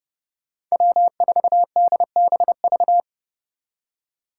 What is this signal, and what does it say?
Morse code, heard as a keyed tone, about 40 wpm, 700 Hz. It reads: W4DBV